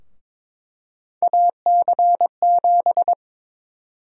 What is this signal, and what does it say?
Morse code, heard as a keyed tone, about 22 wpm, 700 Hz. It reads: AC7
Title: USA Call Sign Prefixes - Repeated 2x - 22wpm